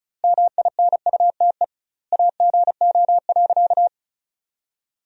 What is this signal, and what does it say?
Morse code, heard as a keyed tone, about 36 wpm, 700 Hz. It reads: MINUTE AGO.
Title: 2x Word-Spacing - Sentences from Top 500 Words - 35wpm